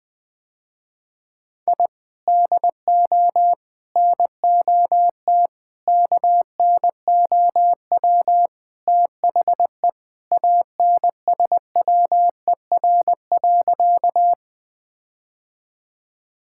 Morse code, heard as a keyed tone, about 20 wpm, 700 Hz. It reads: I DO NOT KNOW THE ANSWER.
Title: Sentences with 5-Word Phrases - Spoken Beginning - 20wpm